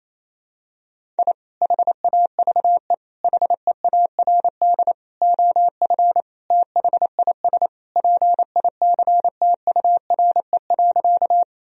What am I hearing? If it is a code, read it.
Morse code, heard as a keyed tone, about 28 wpm, 700 Hz. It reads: I HAVE HEARD OF THIS PICTURE.
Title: Sentences from Top 1000 Words - Review Part 1 - 28wpm